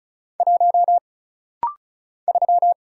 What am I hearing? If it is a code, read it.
Morse code, heard as a keyed tone, about 36 wpm, 700 Hz. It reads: 1 3
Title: Single Letter-Number - Rapid-Fire 35wpm